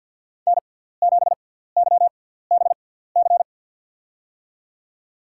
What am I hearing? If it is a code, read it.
Morse code, heard as a keyed tone, about 53 wpm, spaced as for 17 wpm, 700 Hz. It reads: N7YBC
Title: USA Call Signs - 6x Character Spacing - 50wpm